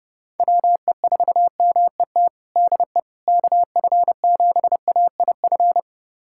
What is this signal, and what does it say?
Morse code, heard as a keyed tone, about 30 wpm, 700 Hz. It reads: WE4MET DE KF7AIF
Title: USA Callsign DE Callsign - 6 Characters - 30wpm